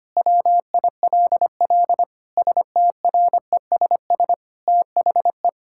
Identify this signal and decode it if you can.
Morse code, heard as a keyed tone, about 25 wpm, 700 Hz. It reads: WILL STRESS THE
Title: Sentences from Top 1000 Words - Review Part 6 - 25wpm